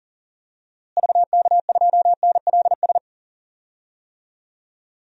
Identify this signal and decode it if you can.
Morse code, heard as a keyed tone, about 40 wpm, 700 Hz. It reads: VK2NLS
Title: DXCC-100 Call Signs - 6 Characters - Repeated 3x - 40wpm